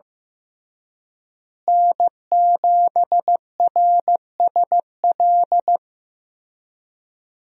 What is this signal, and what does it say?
Morse code, heard as a keyed tone, about 15 wpm, 700 Hz. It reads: N7RSL
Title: USA Call Signs - 5 Characters - Repeated 3x - 15wpm